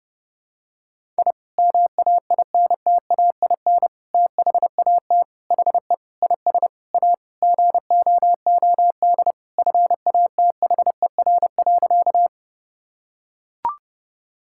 Morse code, heard as a keyed tone, about 30 wpm, 700 Hz. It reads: I MAINTAIN THAT HE IS A GOOD FATHER.
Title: Sentences from Top 1000 Words - Review Part 5 - 30wpm